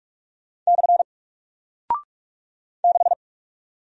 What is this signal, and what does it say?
Morse code, heard as a keyed tone, about 45 wpm, 700 Hz. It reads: / 6
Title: Single Character (Letter:Number:Punctuation:Prosign) Alt 45wpm